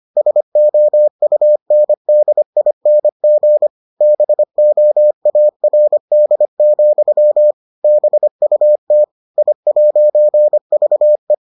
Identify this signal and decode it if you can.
Morse code, heard as a keyed tone, about 25 wpm, 600 Hz. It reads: SOUNDING BOARD, BUT I'VE